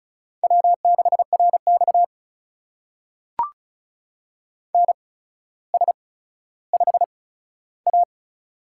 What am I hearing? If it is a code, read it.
Morse code, heard as a keyed tone, about 35 wpm, spaced as for 15 wpm, 700 Hz. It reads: W6RX NS5A
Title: USA Call Signs - 4 Characters - 8x Character Spacing - 35wpm